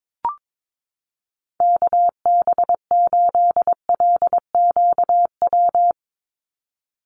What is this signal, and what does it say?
Morse code, heard as a keyed tone, about 22 wpm, 700 Hz. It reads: KB8LQW
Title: USA Call Signs 22wpm